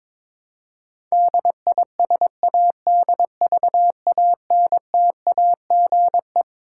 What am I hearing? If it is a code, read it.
Morse code, heard as a keyed tone, about 22 wpm, 700 Hz. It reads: DISADVANTAGE
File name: Twelve Letter Words - 22wpm